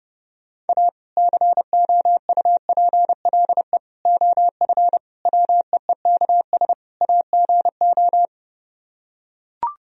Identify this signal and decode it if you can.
Morse code, heard as a keyed tone, about 30 wpm, 700 Hz. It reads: A COUPLE OF WEEKS AGO